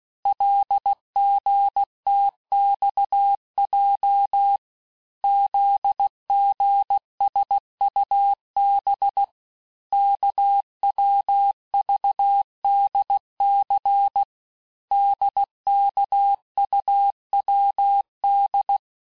Morse code, heard as a keyed tone, about 16 wpm, 800 Hz. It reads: LGTXJ ZGSUB KWVDC DKUWD
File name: cours-du-lundi files / Cours-1586-texte-chiffre-vitesse-15mpm